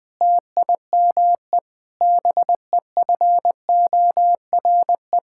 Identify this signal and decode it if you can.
Morse code, heard as a keyed tone, about 20 wpm, 700 Hz. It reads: TIME BEFORE
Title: Sentences with 9-Word Phrases - 20wpm